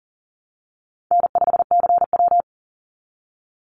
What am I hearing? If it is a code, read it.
Morse code, heard as a keyed tone, about 40 wpm, 700 Hz. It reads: N5CW
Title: Call Signs Repeated 3x - 40wpm